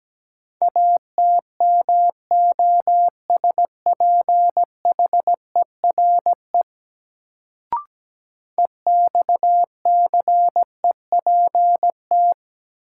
Morse code, heard as a keyed tone, about 17 wpm, 700 Hz. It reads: ATMOSPHERE EXCEPT
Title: Sentences from Top 1600 Words - New Word Drill - 17wpm